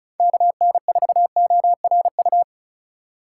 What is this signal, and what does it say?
Morse code, heard as a keyed tone, about 36 wpm, 700 Hz. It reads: KN4ORU
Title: USA Call Signs - 6 Characters - 3x Character Spacing - 35wpm